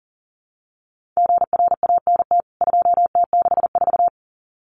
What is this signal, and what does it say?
Morse code, heard as a keyed tone, about 40 wpm, 700 Hz. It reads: GRANT 2T64
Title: CWT Exchange 40wpm